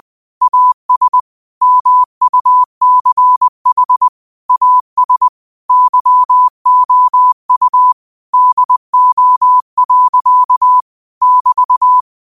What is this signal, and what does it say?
Morse code, heard as a keyed tone, about 20 wpm, 1000 Hz. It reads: AS MUCH AS YOU DO. <BT>
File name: qotdfiles / 20wpm